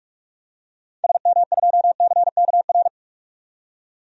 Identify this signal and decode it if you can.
Morse code, heard as a keyed tone, about 46 wpm, 700 Hz. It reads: SM2CKR
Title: DXCC-100 Call Signs - 6 Characters - 45wpm